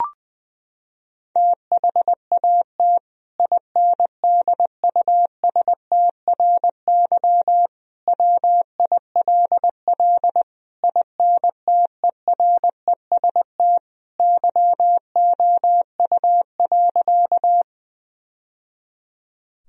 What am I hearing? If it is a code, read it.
Morse code, heard as a keyed tone, about 21 wpm, 700 Hz. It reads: THAT INDUSTRY WILL INTEREST YOU.
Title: Sentences from Top 1000 Words - Review Part 4 - 20wpm